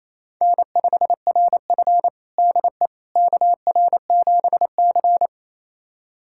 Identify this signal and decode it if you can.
Morse code, heard as a keyed tone, about 28 wpm, 700 Hz. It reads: N5RF DE KR7C
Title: USA Callsign DE Callsign - 4 Characters - 28wpm